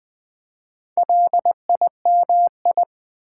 Morse code, heard as a keyed tone, about 20 wpm, 700 Hz. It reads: LIMI
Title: Build A Word - 1 Letter At A Time - 5-Letter Words - 20wpm